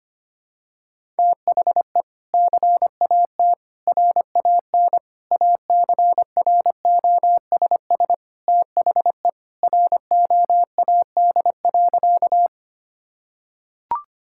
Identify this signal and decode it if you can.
Morse code, heard as a keyed tone, about 25 wpm, 700 Hz. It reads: THE CAT RAN ACROSS THE ROAD.
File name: practice / 25wpm / Sentences with 2-Word Phrases - 25wpm